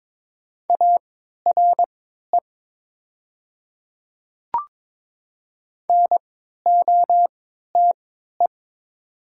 Morse code, heard as a keyed tone, about 22 wpm, spaced as for 12 wpm, 700 Hz. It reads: ARE NOTE
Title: 3x Character Spacing - Top 500 Words - 22wpm